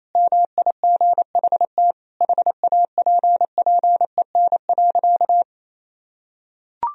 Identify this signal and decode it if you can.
Morse code, heard as a keyed tone, about 28 wpm, 700 Hz. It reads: MIGHT HAPPEN.
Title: Sentences from Top 400 Words and Abbreviations 28wpm